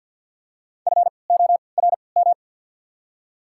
Morse code, heard as a keyed tone, about 54 wpm, spaced as for 29 wpm, 700 Hz. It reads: FORM
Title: 3x Character Spacing - Top 500 Words - 50wpm